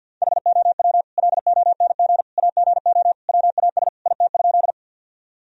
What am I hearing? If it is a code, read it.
Morse code, heard as a keyed tone, about 50 wpm, 700 Hz. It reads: HOW LONG AGO WAS IT?
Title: Sentences from Top 1000 Words - Review Part 3 - 50wpm